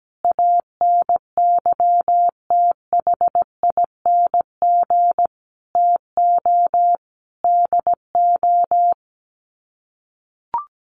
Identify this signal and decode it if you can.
Morse code, heard as a keyed tone, about 17 wpm, 700 Hz. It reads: ANYTHING TO DO
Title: Sets of 5 Words Encore 17wpm